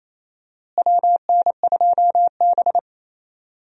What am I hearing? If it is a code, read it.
Morse code, heard as a keyed tone, about 28 wpm, 700 Hz. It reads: WN2B
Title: USA Call Signs - 4 Characters - Repeated 3x - 28wpm